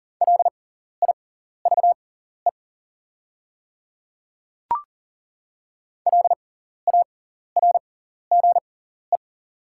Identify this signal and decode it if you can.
Morse code, heard as a keyed tone, about 39 wpm, spaced as for 14 wpm, 690 Hz. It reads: LIVE LARGE
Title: 6x Character Spacing - Top 300 Words - 40wpm